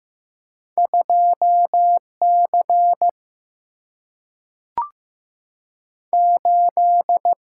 Morse code, heard as a keyed tone, about 15 wpm, 700 Hz. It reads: 2 C 8
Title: Single-Letter Single-Number 15wpm